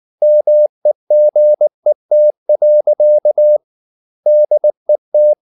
Morse code, heard as a keyed tone, about 19 wpm, 600 Hz. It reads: MEGET. DET
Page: Ferie 19 wpm